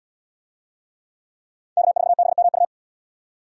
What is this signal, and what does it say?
Morse code, heard as a keyed tone, about 69 wpm, 700 Hz. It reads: K4BMU